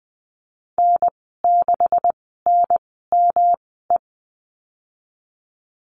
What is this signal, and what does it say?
Morse code, heard as a keyed tone, about 20 wpm, spaced as for 15 wpm, 700 Hz. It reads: N6NME
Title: 128 Lesson 21g - Number 6 - Call Signs 20wpm